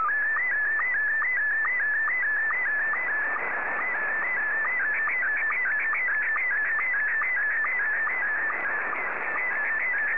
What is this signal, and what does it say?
SSTV (slow-scan television), Scottie 1.